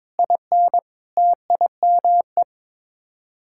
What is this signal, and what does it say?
Morse code, heard as a keyed tone, about 22 wpm, 700 Hz. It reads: IN TIME
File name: practice / 22wpm / Sets of 8 Words - 22wpm